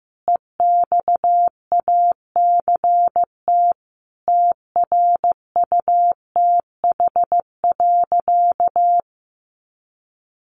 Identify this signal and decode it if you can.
Morse code, heard as a keyed tone, about 15 wpm, 700 Hz. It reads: EXACT TRUTH.
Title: Sentences from Top 1300 Words 15wpm